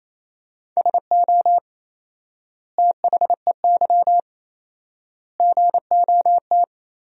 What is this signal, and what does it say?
Morse code, heard as a keyed tone, about 28 wpm, 700 Hz. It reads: SO THEY GOT
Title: 4x Word-Spacing - Sentences from Top 300 Words - 28wpm